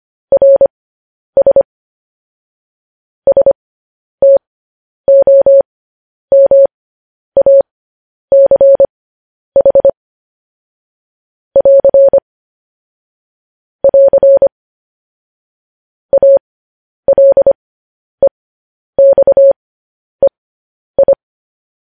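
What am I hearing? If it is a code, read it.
Morse code, heard as a keyed tone, about 25 wpm, spaced as for 10 wpm, 550 Hz. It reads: RS STOMACH <AR> <AR> ALEXEI
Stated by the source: Morse code news headlines.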